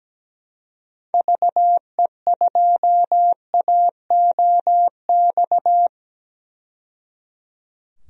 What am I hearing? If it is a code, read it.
Morse code, heard as a keyed tone, about 17 wpm, 700 Hz. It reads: VE2AOX
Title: DXCC-100 Call Signs - 6 Characters - 17wpm